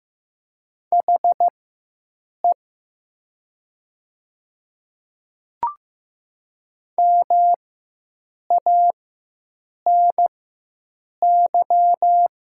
Morse code, heard as a keyed tone, about 15 wpm, spaced as for 7 wpm, 700 Hz. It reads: HE MANY
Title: 4x Character Spacing - Top 100 Words - 15wpm